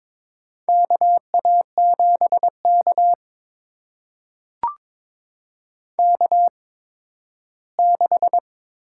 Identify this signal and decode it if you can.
Morse code, heard as a keyed tone, about 22 wpm, spaced as for 13 wpm, 700 Hz. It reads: KA7K K6
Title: USA Call Signs - 4 Characters - 8x Character Spacing - 22wpm